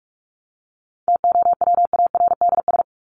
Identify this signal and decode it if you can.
Morse code, heard as a keyed tone, about 44 wpm, 700 Hz. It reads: TOWARDS
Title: ES:S Suffix - Top 100 Words with Base Words 45wpm